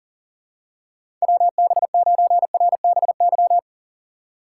Morse code, heard as a keyed tone, about 40 wpm, 700 Hz. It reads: WB9RBY
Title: USA Call Signs - 2x Character Spacing - 40wpm